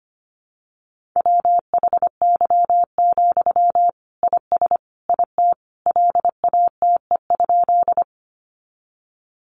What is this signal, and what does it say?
Morse code, heard as a keyed tone, about 25 wpm, 700 Hz. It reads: WHY, IS IT LATE?
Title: Sentences from Top 300 Words 25wpm